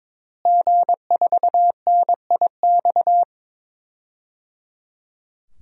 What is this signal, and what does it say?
Morse code, heard as a keyed tone, about 22 wpm, 700 Hz. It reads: G4NIX
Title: DXCC-100 Call Signs - 5 Characters - Repeated 3x - 22wpm